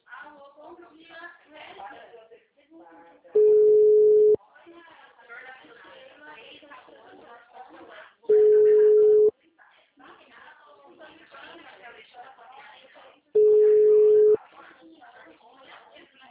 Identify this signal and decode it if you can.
Ringback tone (Europe, 425 Hz), with speech.
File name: out- -2017- - - .12662